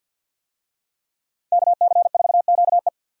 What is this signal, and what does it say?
Morse code, heard as a keyed tone, about 48 wpm, 700 Hz. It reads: KC4QE
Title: USA Call Signs - 5 Characters - 50wpm